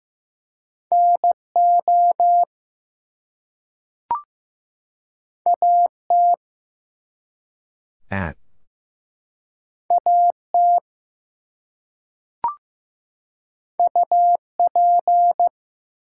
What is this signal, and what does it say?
Morse code, heard as a keyed tone, about 15 wpm, 700 Hz. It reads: NO AT AT UP
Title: Two Letter Words - 15wpm